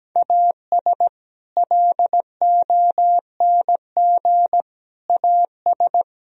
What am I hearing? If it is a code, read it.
Morse code, heard as a keyed tone, about 17 wpm, 700 Hz. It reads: AS LONG AS
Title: Sentences from Top 1000 Words - Review Part 2 - 17wpm